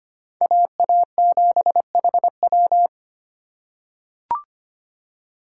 Morse code, heard as a keyed tone, about 25 wpm, 700 Hz. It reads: AA7HW
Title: USA Call Signs - 5 Characters - 3x Character Spacing - 25wpm